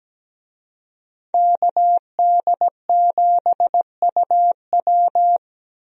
Morse code, heard as a keyed tone, about 17 wpm, 700 Hz. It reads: KD7UW
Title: USA Call Signs - 5 Characters - 17wpm